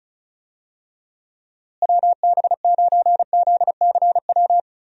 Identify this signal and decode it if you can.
Morse code, heard as a keyed tone, about 36 wpm, 700 Hz. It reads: WB9ZCW
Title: USA Call Signs - 6 Characters - Repeated 2x - 35wpm